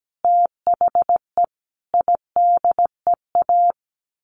Morse code, heard as a keyed tone, about 17 wpm, 700 Hz. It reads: THE IDEA
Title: Sets of 5 Words Encore2 17wpm